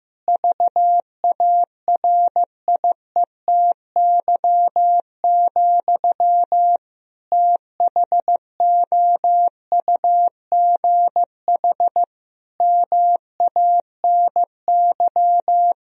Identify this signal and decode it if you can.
Morse code, heard as a keyed tone, about 15 wpm, 700 Hz. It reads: VARIETY, THOUGH MANY
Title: Sentences from Top 1000 Words - Review Part 4 - 15wpm